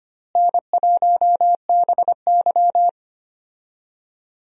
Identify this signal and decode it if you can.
Morse code, heard as a keyed tone, about 25 wpm, 700 Hz. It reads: N1BY
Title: USA Call Signs - 4 Characters - Repeated 2x - 25wpm